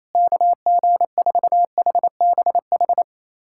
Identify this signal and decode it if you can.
Morse code, heard as a keyed tone, about 28 wpm, 700 Hz. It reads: KG4HBH
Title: USA Callsign DE Callsign 28wpm